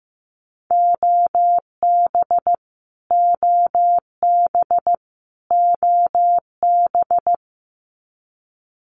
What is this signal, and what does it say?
Morse code, heard as a keyed tone, about 15 wpm, 700 Hz. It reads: OB OB OB
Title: Abbreviations Repeated 3x - 15wpm